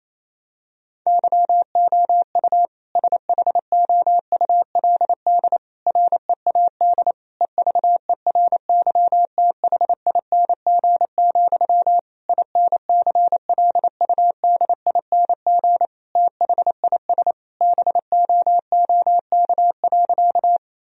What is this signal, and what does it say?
Morse code, heard as a keyed tone, about 28 wpm, 700 Hz. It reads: YOU SHOULD READ EVERYTHING, INCLUDING THIS BOOK.
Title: Sentences from Top 1000 Words - Review Part 4 - 28wpm